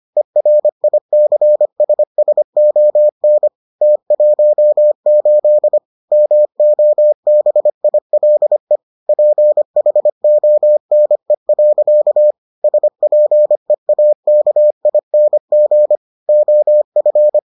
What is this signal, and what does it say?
Morse code, heard as a keyed tone, about 25 wpm, 600 Hz. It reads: ERICSSON T18 MOBILE PHONE. SPEAKING OF